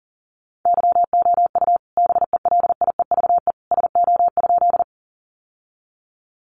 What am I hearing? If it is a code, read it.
Morse code, heard as a keyed tone, about 40 wpm, 700 Hz. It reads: YOU BELIEVE SO?
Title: Sentences from Top 1000 Words 40wpm